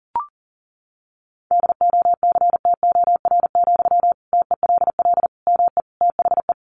Morse code, heard as a keyed tone, about 40 wpm, 700 Hz. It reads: DOCTOR, TELL ME THE
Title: Sentences from Top 1500 Words 40wpm